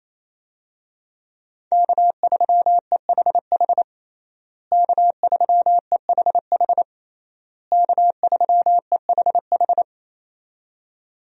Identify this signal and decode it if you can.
Morse code, heard as a keyed tone, about 27 wpm, 700 Hz. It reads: K3EHH K3EHH K3EHH
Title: USA Call Signs - 5 Characters - Repeated 3x - 28wpm